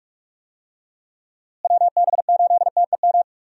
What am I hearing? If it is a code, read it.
Morse code, heard as a keyed tone, about 45 wpm, 700 Hz. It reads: WB8TEM
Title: USA Call Signs - 6 Characters - Repeated 3x - 45wpm